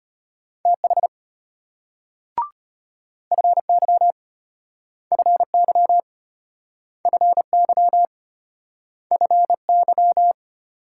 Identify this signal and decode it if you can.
Morse code, heard as a keyed tone, about 32 wpm, 700 Hz. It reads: TH FY FY FY FY